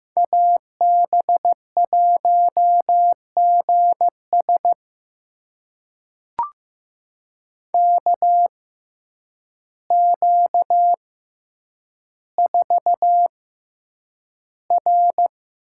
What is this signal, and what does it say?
Morse code, heard as a keyed tone, about 15 wpm, spaced as for 8 wpm, 700 Hz. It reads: AB1GS KQ4R
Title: USA Call Signs - 5 Characters - 6x Character Spacing - 15wpm